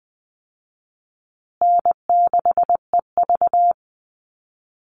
Morse code, heard as a keyed tone, about 20 wpm, 700 Hz. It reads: N6EV
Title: Call Signs Repeated 3x - 20wpm